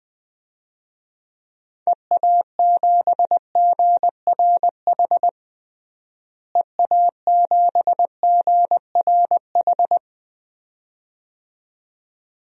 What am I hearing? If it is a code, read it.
Morse code, heard as a keyed tone, about 20 wpm, 700 Hz. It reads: EA7GRH EA7GRH
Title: DXCC-100 Call Signs - 6 Characters - Repeated 2x - 20wpm